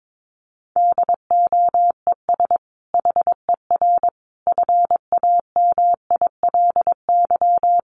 Morse code, heard as a keyed tone, about 22 wpm, 700 Hz. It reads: DOES HER FAMILY
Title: Sentences from Top 1400 Words 22wpm